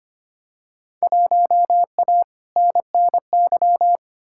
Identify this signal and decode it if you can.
Morse code, heard as a keyed tone, about 25 wpm, 700 Hz. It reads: 1A NNY
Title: Field Day Exchanges - 25wpm